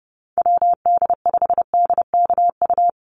Morse code, heard as a keyed tone, about 30 wpm, 700 Hz. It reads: WD5DKU
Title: USA Call Signs Repeated 3x 30wpm